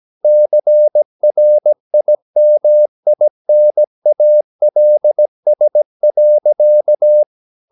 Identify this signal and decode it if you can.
Morse code, heard as a keyed tone, about 17 wpm, 600 Hz. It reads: CRIMINALS.
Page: Romanen Never af Ken Follet på engelsk.